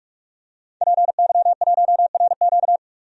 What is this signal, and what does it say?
Morse code, heard as a keyed tone, about 43 wpm, 700 Hz. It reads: PY1RQ
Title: DXCC-100 Call Signs - 5 Characters - 45wpm